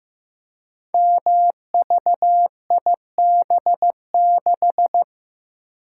Morse code, heard as a keyed tone, about 15 wpm, 700 Hz. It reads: MVIB6
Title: Code Groups - 5 Letter:Numbers - 15wpm